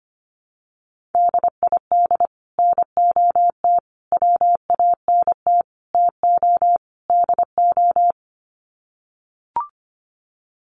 Morse code, heard as a keyed tone, about 25 wpm, 700 Hz. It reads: DID NOT WANT TO DO
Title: Sets of 5 Words Encore 25wpm